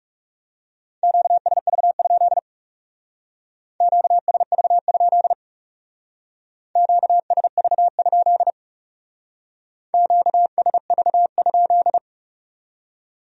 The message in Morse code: QSV? QSV? QSV? QSV?

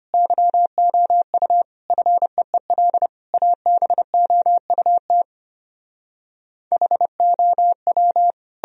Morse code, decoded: YOU FEEL ABOUT HOW

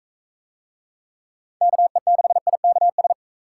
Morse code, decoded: KE6IKS

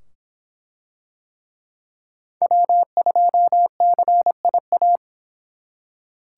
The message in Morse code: W2CIA

